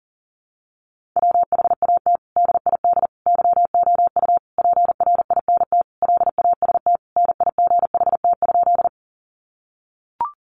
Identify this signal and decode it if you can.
Morse code, heard as a keyed tone, about 40 wpm, 700 Hz. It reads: WHAT DID YOU PRINT LAST NIGHT?
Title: Sentences from Top 1400 Words 40wpm